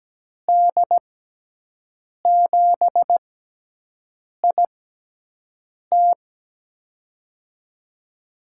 Morse code, heard as a keyed tone, about 17 wpm, spaced as for 6 wpm, 700 Hz. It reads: D7IT
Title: USA Call Signs - 6x Character Spacing - 17wpm